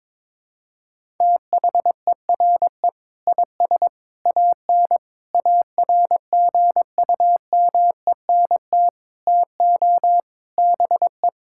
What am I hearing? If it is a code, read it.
Morse code, heard as a keyed tone, about 22 wpm, 700 Hz. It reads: THERE IS AN ARGUMENT TO BE
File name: practice / 22wpm / Sentences with 8-Word Phrases - 22wpm